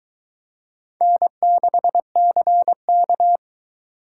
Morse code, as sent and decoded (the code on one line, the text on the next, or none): -. -.... -.-. -.-
N6CK